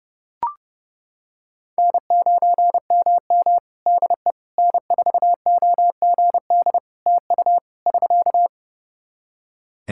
Morse code, heard as a keyed tone, about 30 wpm, 700 Hz. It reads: N9MM DE N4OGD TU <SK>
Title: K1USN SST QSO Practice 30wpm